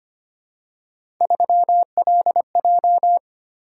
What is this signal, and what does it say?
Morse code, heard as a keyed tone, about 25 wpm, 700 Hz. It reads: 3LJ
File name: Three Letter-Number Combinations - 25wpm